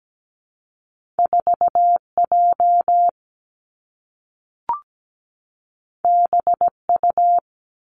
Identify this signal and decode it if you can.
Morse code, heard as a keyed tone, about 17 wpm, 700 Hz. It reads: 4J BU
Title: Two Challenging Letter-Number Combinations 17wpm